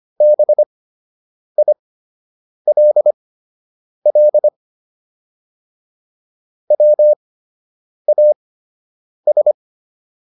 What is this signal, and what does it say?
Morse code, heard as a keyed tone, about 25 wpm, spaced as for 8 wpm, 600 Hz. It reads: BILL WAS